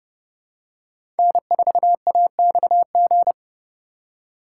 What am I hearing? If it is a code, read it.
Morse code, heard as a keyed tone, about 30 wpm, 700 Hz. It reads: N4AXG
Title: USA Call Signs - 5 Characters - 3x Character Spacing - 30wpm